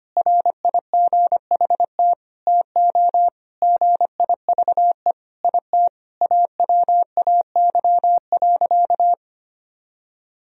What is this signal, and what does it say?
Morse code, heard as a keyed tone, about 25 wpm, 700 Hz. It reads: RIGHT TO GIVE IT AWAY.
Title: Sentences from Top 1000 Words - Review Part 1 - 25wpm